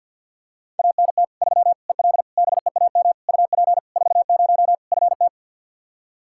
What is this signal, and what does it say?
Morse code, heard as a keyed tone, about 45 wpm, 700 Hz. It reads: ANT 3 EL BEAM UP 40 FT
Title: Practice QSO - Exchange 2 - Info - 50wpm